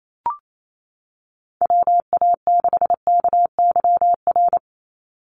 Morse code, heard as a keyed tone, about 28 wpm, 700 Hz. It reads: WA6KYR
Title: NA SOTA Chaser Callsigns 28wpm